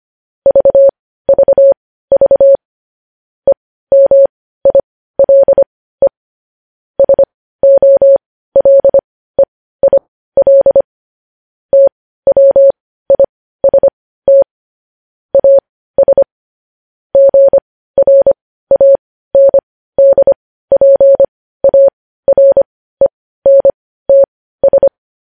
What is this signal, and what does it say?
Morse code, heard as a keyed tone, about 25 wpm, spaced as for 15 wpm, 550 Hz. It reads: VVV EMILE SOLEIL TWIST AS GRANDPARENTS
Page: news-headlines-in-morse-code-at-15-wpm